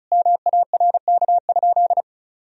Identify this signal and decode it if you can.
Morse code, heard as a keyed tone, about 36 wpm, 700 Hz. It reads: MARK?
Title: Sentences from Top 300 Words and Abbreviations 35wpm